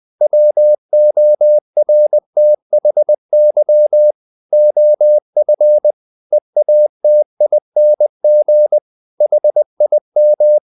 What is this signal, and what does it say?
Morse code, heard as a keyed tone, about 20 wpm, 600 Hz. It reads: WORTHY OF EATING HIM
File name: the old man and the sea part 2 20wpm